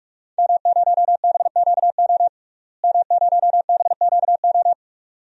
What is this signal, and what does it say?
Morse code, heard as a keyed tone, about 47 wpm, 700 Hz. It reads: M0BQO M0BQO
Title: DXCC-100 Call Signs - 5 Characters - Repeated 2x - 45wpm